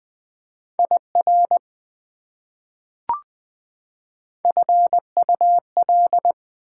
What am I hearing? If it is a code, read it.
Morse code, heard as a keyed tone, about 20 wpm, 700 Hz. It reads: IR FUL
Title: Prefixes and Suffixes 20wpm